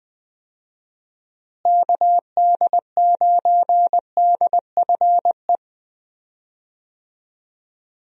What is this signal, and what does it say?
Morse code, heard as a keyed tone, about 20 wpm, 700 Hz. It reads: KD9DFE
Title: USA Call Signs - 6 Characters - Repeated 2x - 20wpm